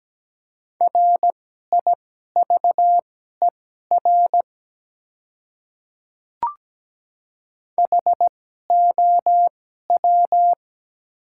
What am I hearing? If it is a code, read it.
Morse code, heard as a keyed tone, about 17 wpm, spaced as for 12 wpm, 700 Hz. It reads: RIVER HOW
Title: 2x Character Spacing - Top 300 Words - 17wpm